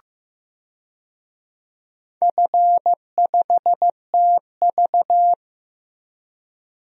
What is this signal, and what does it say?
Morse code, heard as a keyed tone, about 15 wpm, 700 Hz. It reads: F5TV
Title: DXCC-100 Call Signs - 4 Characters - 15wpm